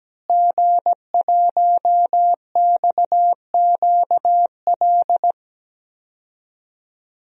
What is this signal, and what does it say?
Morse code, heard as a keyed tone, about 17 wpm, 700 Hz. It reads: G1XQL
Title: DXCC-100 Call Signs - 5 Characters - Repeated 3x - 17wpm